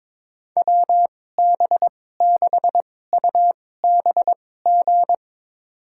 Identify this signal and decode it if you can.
Morse code, heard as a keyed tone, about 22 wpm, spaced as for 16 wpm, 700 Hz. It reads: WB6UBG
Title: USA Call Signs - 6 Characters - 2x Character Spacing - 22wpm